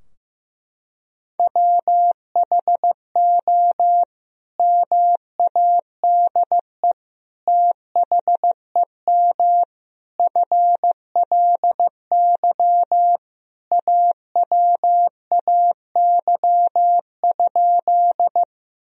Morse code, heard as a keyed tone, about 14 wpm, 700 Hz. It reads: WHO MADE THEM FLY AWAY?
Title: Sentences from Top 1000 Words - Review Part 2 - 15wpm